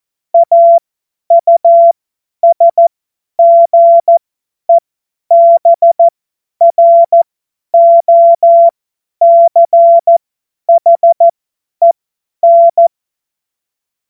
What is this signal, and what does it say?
Morse code, heard as a keyed tone, about 14 wpm, spaced as for 10 wpm, 680 Hz. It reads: AUSGEBROCHEN
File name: Morsekurs / Witze / Witz 09. November 2021, 50 F70